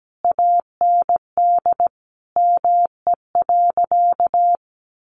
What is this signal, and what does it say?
Morse code, heard as a keyed tone, about 17 wpm, 700 Hz. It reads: AND ME.
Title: Sentences from Top 500 Words 17wpm